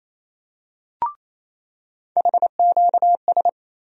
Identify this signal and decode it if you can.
Morse code, heard as a keyed tone, about 27 wpm, 700 Hz. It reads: HQS
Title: USA Call Sign Postfixes - 3 Letters 28wpm